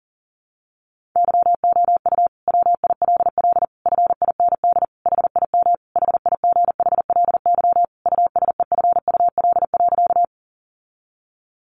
Morse code, heard as a keyed tone, about 37 wpm, 700 Hz. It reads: YOU WILL FIND HIM HIGHLY USEFUL.
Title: Sentences from Top 1100 Words 40wpm